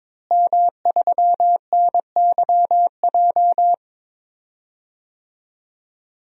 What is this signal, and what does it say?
Morse code, heard as a keyed tone, about 22 wpm, 700 Hz. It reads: M3NYJ
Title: DXCC-100 Call Signs - 5 Characters - Repeated 2x - 22wpm